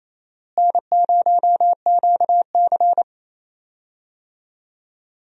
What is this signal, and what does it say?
Morse code, heard as a keyed tone, about 28 wpm, 700 Hz. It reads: N0QC
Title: USA Call Signs - 4 Characters - Repeated 3x - 28wpm